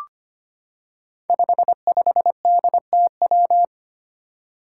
Morse code, heard as a keyed tone, about 25 wpm, 700 Hz. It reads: 55DTW
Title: Five Challenging Letter-Number Combinations 25wpm